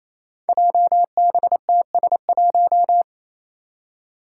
Morse code, heard as a keyed tone, about 28 wpm, 700 Hz. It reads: JBTS1
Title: Five Challenging Letter-Number Combinations 28wpm